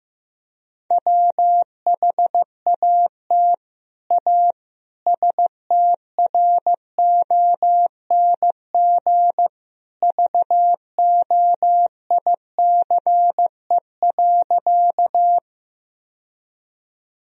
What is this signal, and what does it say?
Morse code, heard as a keyed tone, about 16 wpm, 700 Hz. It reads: WHAT A STRONG VOICE.
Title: Sentences from Top 1000 Words - Review Part 3 - 15wpm